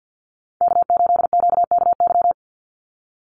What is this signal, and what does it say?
Morse code, heard as a keyed tone, about 50 wpm, 700 Hz. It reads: K8QKY
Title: Call Signs - 50wpm